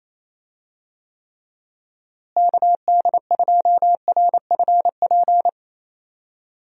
Morse code, heard as a keyed tone, about 27 wpm, 700 Hz. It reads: KD2RFP
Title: USA Call Signs - 6 Characters - Repeated 3x - 28wpm